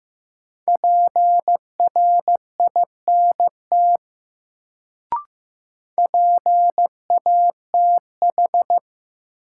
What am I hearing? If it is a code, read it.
Morse code, heard as a keyed tone, about 15 wpm, 700 Hz. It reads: PRINT PATH
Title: Sentences from Top 1400 Words - New Word Drill Alt - 15wpm